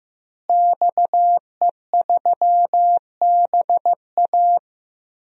Morse code, heard as a keyed tone, about 15 wpm, 700 Hz. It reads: XE3BA
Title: DXCC-100 Call Signs - 5 Characters - Repeated 3x - 15wpm